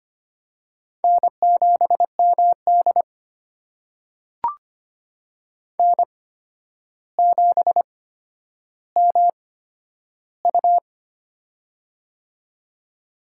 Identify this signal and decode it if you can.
Morse code, heard as a keyed tone, about 25 wpm, spaced as for 11 wpm, 700 Hz. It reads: N7MD N7MU
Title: USA Call Signs - 4 Characters - 8x Character Spacing - 25wpm